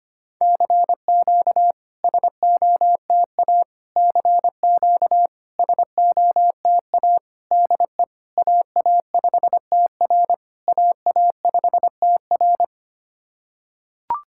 Morse code, heard as a keyed tone, about 25 wpm, 690 Hz. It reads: CQ SOTA CQ SOTA DE AA5TR AA5TR
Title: SOTA QSO Practice 25wpm